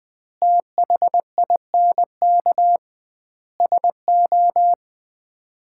Morse code, heard as a keyed tone, about 20 wpm, 700 Hz. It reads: THINK SO
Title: 2x Word-Spacing - Sets of 4 Words - 20wpm